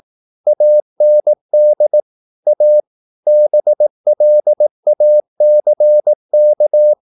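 Morse code, heard as a keyed tone, about 18 wpm, 600 Hz. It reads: AND A BLACK
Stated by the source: Morse practice at 18 wpm, English text, from CW med Gnister.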